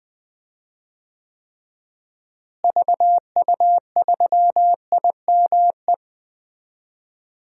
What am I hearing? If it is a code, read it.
Morse code, heard as a keyed tone, about 20 wpm, 700 Hz. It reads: VU3IME
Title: DXCC-100 Call Signs - 6 Characters - Repeated 3x - 20wpm